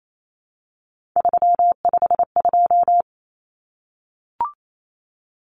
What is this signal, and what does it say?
Morse code, heard as a keyed tone, about 28 wpm, 700 Hz. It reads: 352
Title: Numbers - 3-Digit 28wpm